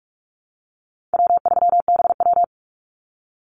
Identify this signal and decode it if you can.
Morse code, heard as a keyed tone, about 44 wpm, 700 Hz. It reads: W3BW
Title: Easy Call Signs - 45wpm